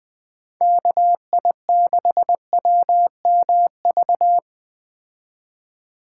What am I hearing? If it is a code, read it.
Morse code, heard as a keyed tone, about 20 wpm, 700 Hz. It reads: KI6WMV
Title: Speed Racing - USA 2x3 Call Signs 17wpm